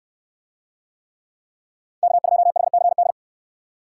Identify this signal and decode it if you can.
Morse code, heard as a keyed tone, about 68 wpm, 700 Hz. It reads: K2HCD